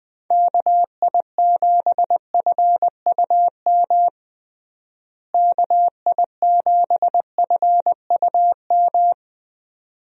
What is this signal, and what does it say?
Morse code, heard as a keyed tone, about 21 wpm, 700 Hz. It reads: KI7FUM KI7FUM
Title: USA Call Signs - 6 Characters - Repeated 2x - 20wpm